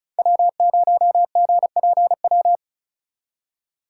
Morse code, heard as a keyed tone, about 44 wpm, 700 Hz. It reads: W0GPW